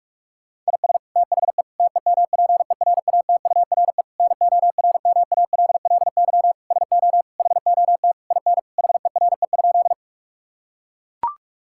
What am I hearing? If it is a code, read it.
Morse code, heard as a keyed tone, about 41 wpm, 700 Hz. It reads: IS THE TEMPERATURE NORMALLY SO HOT IN HERE?
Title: Sentences from Top 1000 Words - Review Part 6 - 45wpm